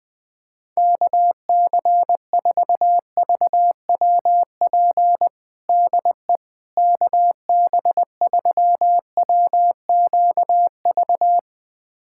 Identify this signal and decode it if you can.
Morse code, heard as a keyed tone, about 20 wpm, 700 Hz. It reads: KC4VWP DE KB3WQV
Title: USA Callsign DE Callsign 20wpm